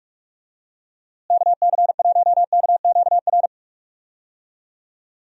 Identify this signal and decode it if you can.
Morse code, heard as a keyed tone, about 46 wpm, 700 Hz. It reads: KC1KQR
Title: USA Call Signs - 6 Characters - Repeated 2x - 45wpm